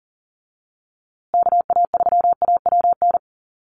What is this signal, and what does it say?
Morse code, heard as a keyed tone, about 40 wpm, 700 Hz. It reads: KA3AWN
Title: USA Call Signs Repeated 2x 40wpm